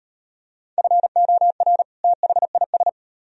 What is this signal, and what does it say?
Morse code, heard as a keyed tone, about 36 wpm, 700 Hz. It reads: FOR THIS